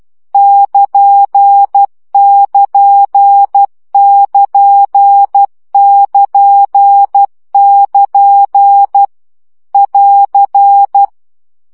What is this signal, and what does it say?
Morse code, heard as a keyed tone, about 12 wpm, 800 Hz. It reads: <KN><KN><KN><KN><KN> <AR>
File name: CW gyakorlas nyito 10 baud